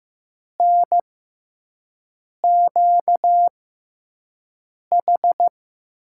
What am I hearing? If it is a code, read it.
Morse code, heard as a keyed tone, about 15 wpm, spaced as for 5 wpm, 700 Hz. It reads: NQH
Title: USA Call Signs - 6 Characters - 6x Character Spacing - 15wpm